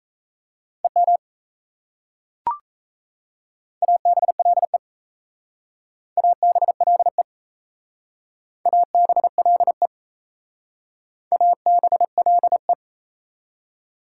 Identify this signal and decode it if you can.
Morse code, heard as a keyed tone, about 34 wpm, 700 Hz. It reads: EM ABLE ABLE ABLE ABLE
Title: Speed-Racing Prefixes and Suffixes 28wpm